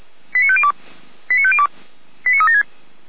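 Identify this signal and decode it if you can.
Five-tone selective call (ZVEI), heard as quick runs of five tones, tones of about 70 ms. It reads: 89502 89502 89376
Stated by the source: Received on SDR, ZVEI five-tone call 89115.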